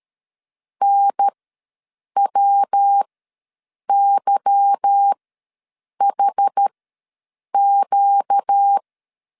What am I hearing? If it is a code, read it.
Morse code, heard as a keyed tone, about 13 wpm, spaced as for 7 wpm, 800 Hz. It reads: NWYHQ
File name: LEZIONI / 8 YJ45 / 8 Y ws 12wpm 3409